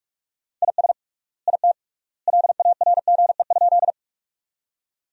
Morse code, heard as a keyed tone, about 42 wpm, 700 Hz. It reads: IS IT LARGE?
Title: 3x Word-Spacing - Sentences from Top 200 Words - 45wpm